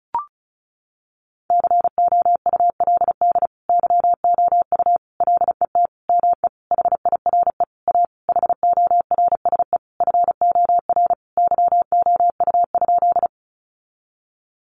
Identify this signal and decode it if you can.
Morse code, heard as a keyed tone, about 34 wpm, 700 Hz. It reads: COULD YOU LET ME HIRE A HORSE FOR YOU?
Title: Sentences from Top 1400 Words 35wpm